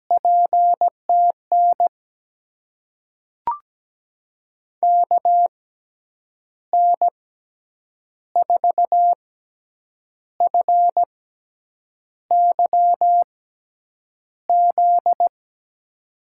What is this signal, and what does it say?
Morse code, heard as a keyed tone, about 17 wpm, spaced as for 7 wpm, 700 Hz. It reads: PTN KN4FYZ
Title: USA Call Signs - 6 Characters - 6x Character Spacing - 17wpm